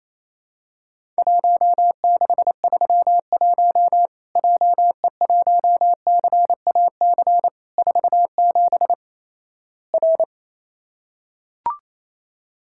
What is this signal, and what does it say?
Morse code, heard as a keyed tone, about 28 wpm, 700 Hz. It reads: 1631 JE1CAC 47 R
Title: Worked All Europe DX Contest - QTC Exchange - 28wpm